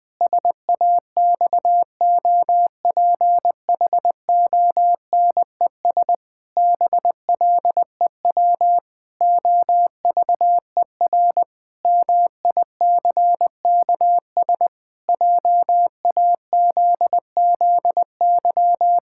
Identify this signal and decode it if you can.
Morse code, heard as a keyed tone, about 20 wpm, 700 Hz. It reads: SAXOPHONES BLEW OVER MICKS JAZZY
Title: Pangram Challenge v2 - 20wpm